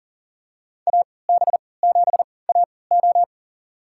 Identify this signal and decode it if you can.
Morse code, heard as a keyed tone, about 40 wpm, spaced as for 23 wpm, 700 Hz. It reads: AB7AO
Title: USA Call Signs - 3x Character Spacing - 40wpm